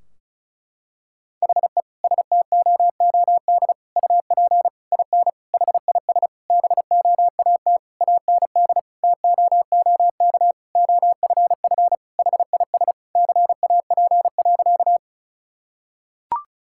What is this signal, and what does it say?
Morse code, heard as a keyed tone, about 34 wpm, 700 Hz. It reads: HE STOOD UP IN HIS BOAT AND TOOK OFF HIS CAP.
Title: Sentences from Top 1600 Words 35wpm